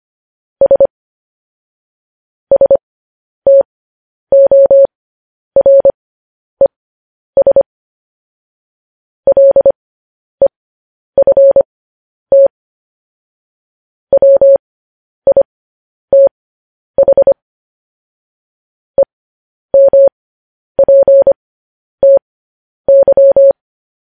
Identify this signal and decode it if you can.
Morse code, heard as a keyed tone, about 25 wpm, spaced as for 10 wpm, 550 Hz. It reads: S STORES LEFT WITH EMPTY
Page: news headlines in morse code at 10 wpm